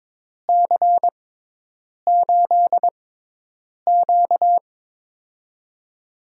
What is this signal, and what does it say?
Morse code, heard as a keyed tone, about 23 wpm, spaced as for 8 wpm, 700 Hz. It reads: C8Q
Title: USA Call Signs - 6x Character Spacing - 22wpm